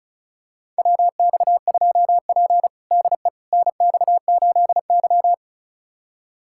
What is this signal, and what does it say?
Morse code, heard as a keyed tone, about 34 wpm, 700 Hz. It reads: WX2P DE NX8Y
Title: USA Callsign DE Callsign - 4 Characters - 35wpm